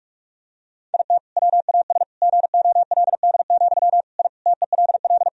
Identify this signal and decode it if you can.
Morse code, heard as a keyed tone, about 46 wpm, 700 Hz. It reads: IT WAS GOLD, I TELL
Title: Sentences from Top 1000 Words - Review Part 3 - 45wpm